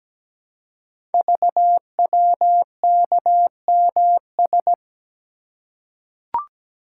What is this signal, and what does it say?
Morse code, heard as a keyed tone, about 17 wpm, 700 Hz. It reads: VWKMS
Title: Code Groups - 5 Letters - 17wpm